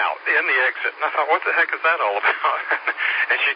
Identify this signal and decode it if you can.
Emergency Alert System (SAME) tones, heard as harsh buzzing bursts, with speech. It is heard from 0.3 to 3.1 s.